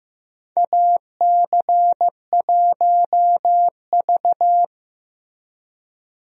Morse code, heard as a keyed tone, about 15 wpm, 700 Hz. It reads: AC1V
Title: USA Call Signs - 4 Characters - Repeated 3x - 15wpm